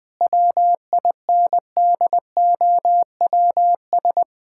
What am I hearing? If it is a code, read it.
Morse code, heard as a keyed tone, about 20 wpm, 700 Hz. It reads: WINDOWS
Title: All-Age Appropriate Jokes - 20wpm